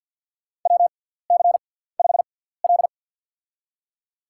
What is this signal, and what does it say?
Morse code, heard as a keyed tone, about 49 wpm, spaced as for 17 wpm, 700 Hz. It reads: WC5L
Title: USA Call Signs - 4 Characters - 6x Character Spacing - 50wpm